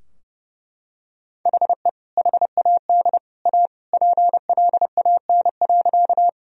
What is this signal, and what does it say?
Morse code, heard as a keyed tone, about 30 wpm, 700 Hz. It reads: HE HAD A PLAN.
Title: Sentences from Top 1000 Words - Review Part 2 - 30wpm